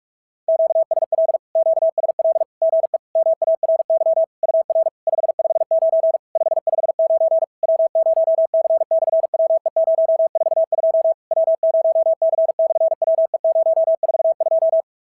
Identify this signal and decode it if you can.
Morse code, heard as a keyed tone, about 44 wpm, 650 Hz. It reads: QSL QSL GE MARY UR 559 559 W0C/WE042 W0C/WE042
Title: SOTA QSO Practice 45wpm